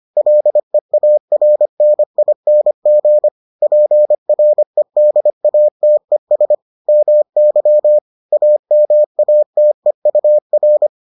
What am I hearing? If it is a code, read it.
Morse code, heard as a keyed tone, about 25 wpm, 600 Hz. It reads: LEARNING PREDATES MY AMATEUR